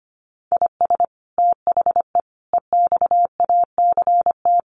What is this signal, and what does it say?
Morse code, heard as a keyed tone, about 25 wpm, 700 Hz. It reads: IS THE EXACT
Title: Sentences from Top 1300 Words 25wpm